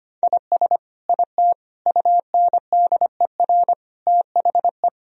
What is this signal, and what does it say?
Morse code, heard as a keyed tone, about 25 wpm, 700 Hz. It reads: IS IT UNDER THE
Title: Sentences from Top 1000 Words - Review Part 1 - 25wpm